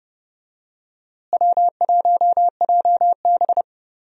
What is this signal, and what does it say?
Morse code, heard as a keyed tone, about 29 wpm, 700 Hz. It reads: W1JB
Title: USA Call Signs - 4 Characters - Repeated 2x - 30wpm